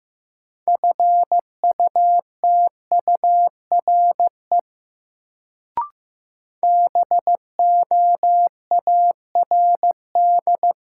Morse code, heard as a keyed tone, about 15 wpm, 700 Hz. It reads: FUTURE BOARD
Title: Sentences from Top 700 Words - New Word Drill - 15wpm